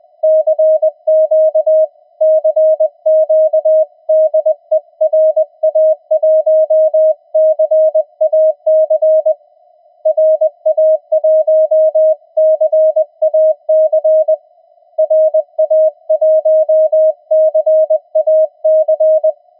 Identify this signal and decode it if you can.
Morse code, heard as a keyed tone, about 20 wpm, 630 Hz. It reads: CQ CQ DE RA1CAC RA1CAC RA1CAC
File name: attachments / TX 7030